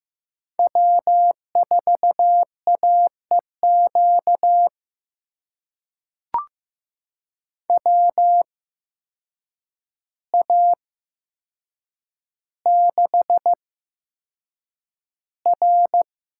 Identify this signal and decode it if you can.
Morse code, heard as a keyed tone, about 15 wpm, spaced as for 7 wpm, 700 Hz. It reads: W4AEQ WA6R
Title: USA Call Signs - 5 Characters - 8x Character Spacing - 15wpm